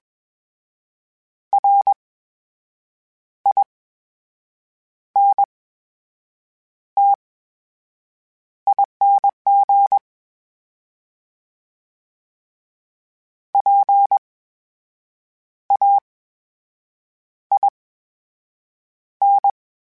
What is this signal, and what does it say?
Morse code, heard as a keyed tone, about 22 wpm, spaced as for 6 wpm, 800 Hz. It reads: RINTING PAIN
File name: Special INGs-mp3s / Special INGs-mp3s / VerySlowToFaster Ing @20